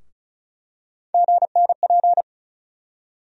Morse code, heard as a keyed tone, about 36 wpm, 700 Hz. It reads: GNP
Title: USA Call Sign Postfixes - 3 Letters 35wpm